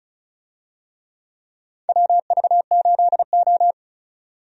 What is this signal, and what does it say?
Morse code, heard as a keyed tone, about 34 wpm, 700 Hz. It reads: WV8O
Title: USA Call Signs - 4 Characters - 35wpm